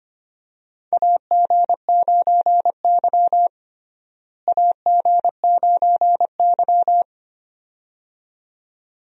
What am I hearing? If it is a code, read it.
Morse code, heard as a keyed tone, about 25 wpm, 700 Hz. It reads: AG9Y AG9Y
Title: USA Call Signs - 4 Characters - Repeated 2x - 25wpm